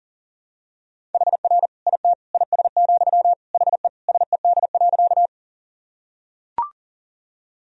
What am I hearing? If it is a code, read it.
Morse code, heard as a keyed tone, about 40 wpm, 700 Hz. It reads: HR IT IS, HE SED.
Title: Sentences from Top 200 Words and Abbreviations 40wpm